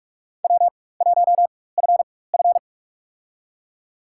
Morse code, heard as a keyed tone, about 45 wpm, spaced as for 21 wpm, 700 Hz. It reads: W1FF
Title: USA Call Signs - 4x Character Spacing - 45wpm